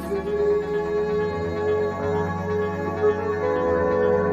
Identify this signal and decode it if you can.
Congestion tone (425 Hz), with noise.